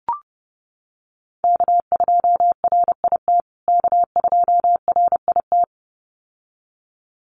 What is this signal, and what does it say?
Morse code, heard as a keyed tone, about 30 wpm, 700 Hz. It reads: K2RIT K2RIT
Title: USA Call Signs Repeated 2x 30wpm